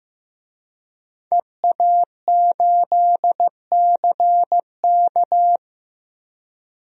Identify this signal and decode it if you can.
Morse code, heard as a keyed tone, about 15 wpm, 700 Hz. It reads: EA8CK
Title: DXCC-100 Call Signs - 5 Characters - Repeated 2x - 15wpm